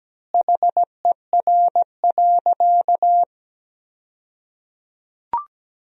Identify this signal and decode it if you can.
Morse code, heard as a keyed tone, about 17 wpm, 700 Hz. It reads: HER.
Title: 2x Word-Spacing - Sentences from Top 200 Words - 17wpm